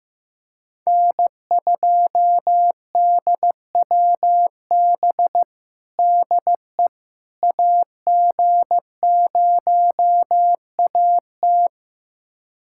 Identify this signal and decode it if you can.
Morse code, heard as a keyed tone, about 15 wpm, 700 Hz. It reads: N2DWB DE AG0AT
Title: USA Callsign DE Callsign - 5 Characters - 15wpm